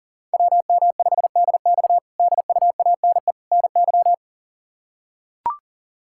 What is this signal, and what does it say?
Morse code, heard as a keyed tone, about 37 wpm, 700 Hz. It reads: WM5DX DUANE NY
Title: K1USN SST QSO Practice - 40wpm